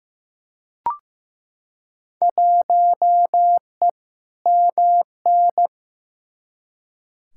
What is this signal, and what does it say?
Morse code, heard as a keyed tone, about 15 wpm, 700 Hz. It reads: 1E MN
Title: Field Day Exchanges - 15wpm